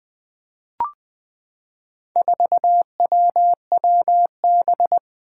Morse code, heard as a keyed tone, about 20 wpm, 700 Hz. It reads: 4WWB
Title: Four Challenging Letter-Number Combinations 20wpm